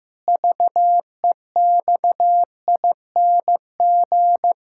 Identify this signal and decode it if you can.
Morse code, heard as a keyed tone, about 15 wpm, 700 Hz. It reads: VEXING
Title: Pangram Challenge v2 - 15wpm